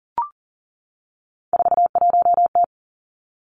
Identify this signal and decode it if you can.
Morse code, heard as a keyed tone, about 38 wpm, 690 Hz. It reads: 41T
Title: Three Challenging Letter-Number Combinations 40wpm